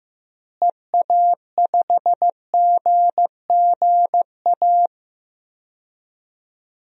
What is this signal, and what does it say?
Morse code, heard as a keyed tone, about 15 wpm, 700 Hz. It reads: EA5GGA
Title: DXCC-100 Call Signs - 6 Characters - Repeated 3x - 15wpm